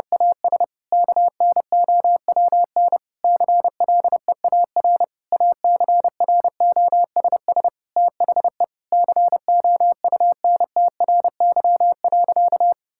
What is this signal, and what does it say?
Morse code, heard as a keyed tone, about 30 wpm, 700 Hz. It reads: AS KNOWN CLEAR ACROSS THE COUNTRY.
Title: Sentences from Top 1000 Words - Review Part 5 - 30wpm